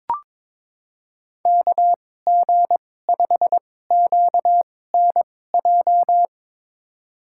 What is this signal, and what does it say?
Morse code, heard as a keyed tone, about 21 wpm, spaced as for 16 wpm, 700 Hz. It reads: KG5QNJ
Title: USA Call Signs - 6 Characters - 2x Character Spacing - 22wpm